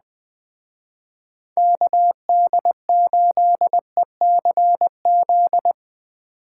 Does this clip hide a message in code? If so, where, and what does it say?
Yes: Morse code from 1.6 to 5.7 s, heard as a keyed tone, about 20 wpm, 700 Hz. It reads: KD8ECZ